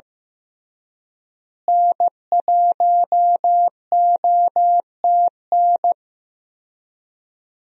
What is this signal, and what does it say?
Morse code, heard as a keyed tone, about 16 wpm, 700 Hz. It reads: N1OTN
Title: USA Call Signs - 5 Characters - Repeated 2x - 15wpm